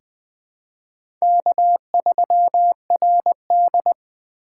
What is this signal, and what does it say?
Morse code, heard as a keyed tone, about 20 wpm, 700 Hz. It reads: K3RD
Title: USA Call Signs - 4 Characters - 3x Character Spacing - 20wpm